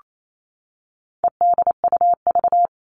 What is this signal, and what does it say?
Morse code, heard as a keyed tone, about 28 wpm, 700 Hz. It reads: EDUV
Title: All Cut Numbers - 4-Digit 28wpm